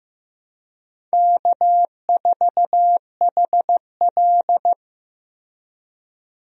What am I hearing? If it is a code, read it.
Morse code, heard as a keyed tone, about 15 wpm, 700 Hz. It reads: K4HL
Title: USA Call Signs - 4 Characters - Repeated 3x - 15wpm